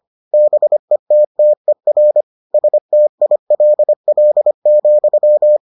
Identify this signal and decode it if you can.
Morse code, heard as a keyed tone, about 25 wpm, 600 Hz. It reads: BETTER STILL,